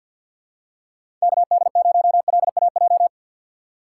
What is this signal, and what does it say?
Morse code, heard as a keyed tone, about 51 wpm, 700 Hz. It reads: KD0LAJ
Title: USA Call Signs - 4x Character Spacing - 50wpm